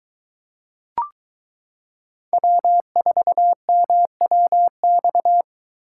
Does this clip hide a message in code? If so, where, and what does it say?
Yes: Morse code from 2.3 to 5.4 s, heard as a keyed tone, about 23 wpm, 700 Hz. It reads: W4MWX